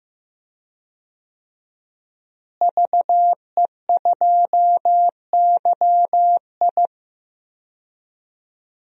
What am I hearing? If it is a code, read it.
Morse code, heard as a keyed tone, about 15 wpm, 700 Hz. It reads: VE2YI
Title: DXCC-100 Call Signs - 5 Characters - Repeated 3x - 15wpm